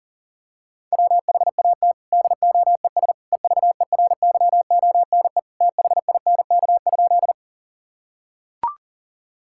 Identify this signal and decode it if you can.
Morse code, heard as a keyed tone, about 40 wpm, 700 Hz. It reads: WHAT DOES EVERYONE THINK?
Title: Sentences from Top 1000 Words - Review Part 5 - 40wpm